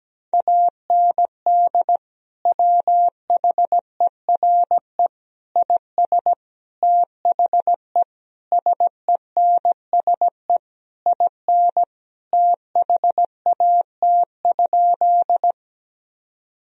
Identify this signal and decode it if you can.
Morse code, heard as a keyed tone, about 17 wpm, 700 Hz. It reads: AND WHERE IS THE SENSE IN THAT?
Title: Sentences from Top 1000 Words - Review Part 4 - 17wpm